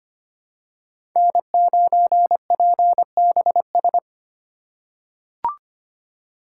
Morse code, heard as a keyed tone, about 25 wpm, 700 Hz. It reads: N9PBS
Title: USA Call Signs - 5 Characters - 2x Character Spacing - 25wpm